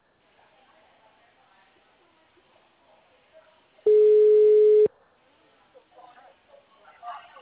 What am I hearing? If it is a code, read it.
Ringback tone (Europe, 425 Hz), with speech.